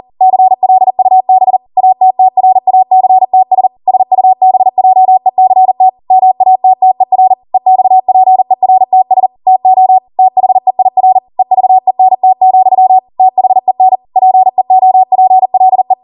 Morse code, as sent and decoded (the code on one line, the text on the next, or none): -.-. .-.. ..- -... / .- - - .-. .- -.-. - ... / ... ..- -... .--- . -.-. - / -- .- - - . .-. / . -..- .--. .-. - ... / - --- / - .... .. .-. / . ...- . -. - --..-- / - .... -. / .--. --- .--. .-.. .
CLUB ATTRACTS SUBJECT MATTER EXPERTS TO THEIR EVENT, THEN PEOPLE